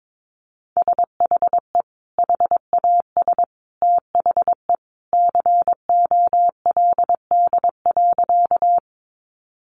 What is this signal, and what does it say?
Morse code, heard as a keyed tone, about 22 wpm, 700 Hz. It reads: SHE HAS THE COLD.
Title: Easy Phrases 22wpm